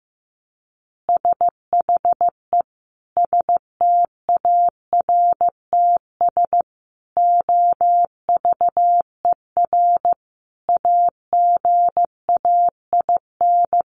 Morse code, heard as a keyed tone, about 15 wpm, 700 Hz. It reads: SHE STARTS OVER AGAIN
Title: Easy Phrases 15wpm